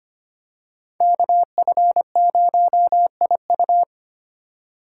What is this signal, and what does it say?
Morse code, heard as a keyed tone, about 25 wpm, 700 Hz. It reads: KF0IU
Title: USA Call Signs - 5 Characters - 4x Character Spacing - 25wpm